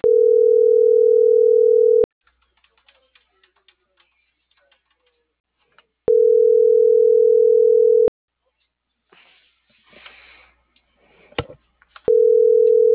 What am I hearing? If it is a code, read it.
Ringback tone (North America), 440 + 480 Hz.